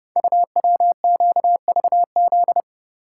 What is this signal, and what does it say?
Morse code, heard as a keyed tone, about 30 wpm, 700 Hz. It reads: UWQVZ
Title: Code Groups - 5 Letters - 30wpm